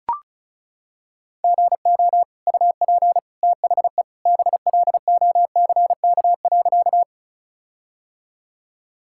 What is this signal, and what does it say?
Morse code, heard as a keyed tone, about 36 wpm, 700 Hz. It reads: GO UP THE BLOCK.
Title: Sentences from Top 1600 Words 35wpm